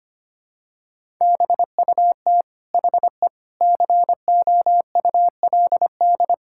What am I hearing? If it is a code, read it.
Morse code, heard as a keyed tone, about 25 wpm, 700 Hz. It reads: BUT HE COULD
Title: Sentences from Top 1600 Words 25wpm